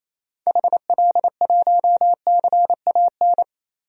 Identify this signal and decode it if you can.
Morse code, heard as a keyed tone, about 28 wpm, 700 Hz. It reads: HL1CAN
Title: DXCC-100 Call Signs - 6 Characters - 28wpm